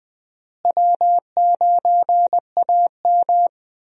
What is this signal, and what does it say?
Morse code, heard as a keyed tone, about 21 wpm, 700 Hz. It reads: W9AM
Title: USA Call Signs - 4 Characters - Repeated 3x - 20wpm